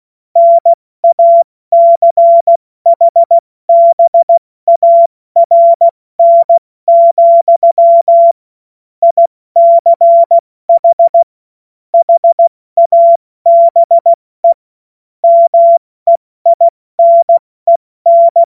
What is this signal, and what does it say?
Morse code, heard as a keyed tone, about 16 wpm, 680 Hz. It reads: NACHBARN, ICH HABE MEINEN
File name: Morsekurs / Witze / Witz 02. Mai 2023 70 F80